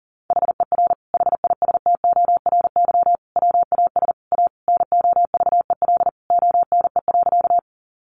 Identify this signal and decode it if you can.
Morse code, heard as a keyed tone, about 40 wpm, 700 Hz. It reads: HER HISTORY WAS A NOVEL ONE.
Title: Sentences from Top 1300 Words 40wpm